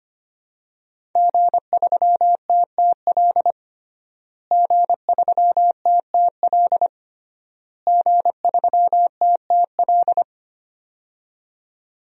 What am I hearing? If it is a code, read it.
Morse code, heard as a keyed tone, about 25 wpm, 700 Hz. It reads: G3TTL G3TTL G3TTL
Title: DXCC-100 Call Signs - 5 Characters - Repeated 3x - 25wpm